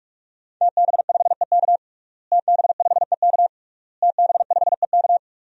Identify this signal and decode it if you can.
Morse code, heard as a keyed tone, about 46 wpm, 700 Hz. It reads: TB5EK TB5EK TB5EK
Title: DXCC-100 Call Signs - 5 Characters - Repeated 3x - 45wpm